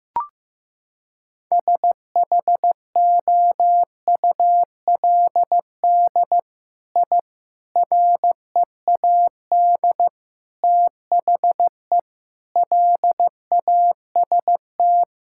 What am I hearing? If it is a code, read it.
Morse code, heard as a keyed tone, about 15 wpm, 700 Hz. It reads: SHOULD I READ THE LAST
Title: Sentences from Top 1000 Words - Review Part 5 - 15wpm